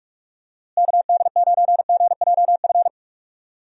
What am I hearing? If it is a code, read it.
Morse code, heard as a keyed tone, about 43 wpm, 700 Hz. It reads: KD9GJF
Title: USA Call Signs - 6 Characters - Repeated 2x - 45wpm